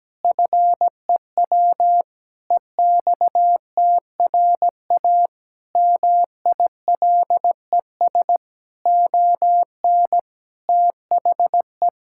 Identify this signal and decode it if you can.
Morse code, heard as a keyed tone, about 17 wpm, 700 Hz. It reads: FEW EXTRA MILES ON THE
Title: Pangram Challenge v2 - 17wpm